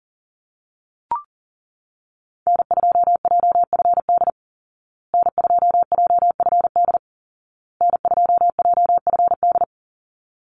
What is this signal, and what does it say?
Morse code, heard as a keyed tone, about 40 wpm, 700 Hz. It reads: N2JFD N2JFD N2JFD
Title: Call Signs Repeated 2x - 40wpm